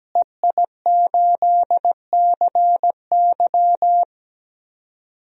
Morse code, heard as a keyed tone, about 17 wpm, 700 Hz. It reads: EI8CY
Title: DXCC-100 Call Signs - 5 Characters - Repeated 3x - 17wpm